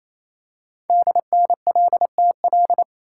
Morse code, heard as a keyed tone, about 27 wpm, 700 Hz. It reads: DNLTL
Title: Code Groups - 5 Letters - 28wpm